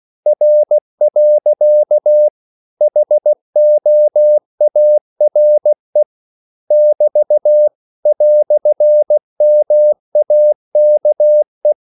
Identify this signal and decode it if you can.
Morse code, heard as a keyed tone, about 16 wpm, 600 Hz. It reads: R. HOARE <BT> "MAKE
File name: CW / Practice / EnglishQuotations0216